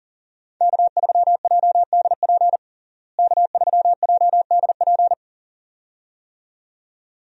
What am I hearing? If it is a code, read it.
Morse code, heard as a keyed tone, about 37 wpm, 700 Hz. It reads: K3JDP K3JDP
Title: USA Call Signs - 5 Characters - Repeated 2x - 40wpm